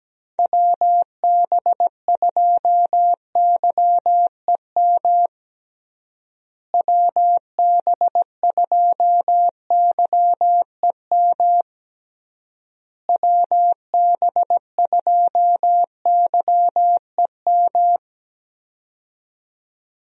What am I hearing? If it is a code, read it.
Morse code, heard as a keyed tone, about 17 wpm, 700 Hz. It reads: WB2YEM WB2YEM WB2YEM
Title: USA Call Signs - 6 Characters - Repeated 3x - 17wpm